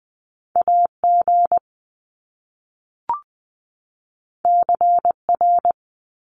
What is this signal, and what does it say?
Morse code, heard as a keyed tone, about 20 wpm, 700 Hz. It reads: A G C R
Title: Top Two Letter Combinations in Words 20wpm